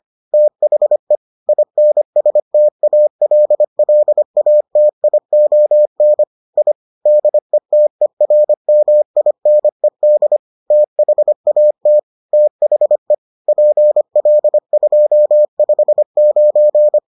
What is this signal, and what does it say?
Morse code, heard as a keyed tone, about 25 wpm, 600 Hz. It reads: THE INSTALLATION I DETERMINED THAT THE PL259